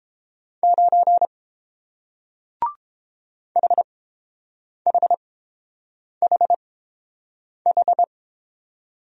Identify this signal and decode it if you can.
Morse code, heard as a keyed tone, about 31 wpm, spaced as for 7 wpm, 700 Hz. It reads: 9 HHHH